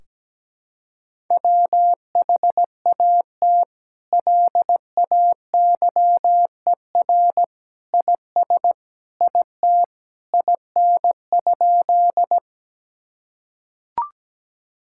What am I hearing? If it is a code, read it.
Morse code, heard as a keyed tone, about 17 wpm, 700 Hz. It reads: WHAT LAYER IS IT IN?
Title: Sentences from Top 1600 Words 17wpm